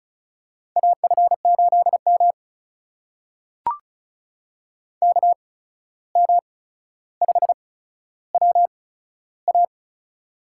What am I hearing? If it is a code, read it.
Morse code, heard as a keyed tone, about 36 wpm, spaced as for 14 wpm, 700 Hz. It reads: AF8M KM5WA
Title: USA Call Signs - 8x Character Spacing - 35wpm